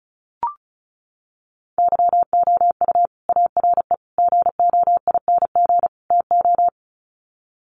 Morse code, heard as a keyed tone, about 34 wpm, 700 Hz. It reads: YOU ARE GOING TO
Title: Sets of 4 Words 35wpm